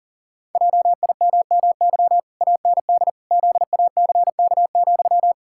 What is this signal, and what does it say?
Morse code, heard as a keyed tone, about 40 wpm, 700 Hz. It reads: JIMMY AND ZACK,
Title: Pangram Challenge v2 - 40wpm